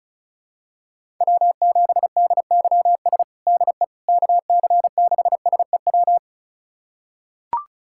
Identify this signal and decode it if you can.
Morse code, heard as a keyed tone, about 34 wpm, 700 Hz. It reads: W7DYS DE KC6SEW
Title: USA Callsign DE Callsign 35wpm